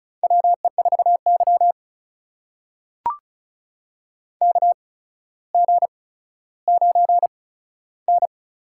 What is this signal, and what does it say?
Morse code, heard as a keyed tone, about 36 wpm, spaced as for 15 wpm, 700 Hz. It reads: WE4Y KG9N
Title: USA Call Signs - 4 Characters - 8x Character Spacing - 35wpm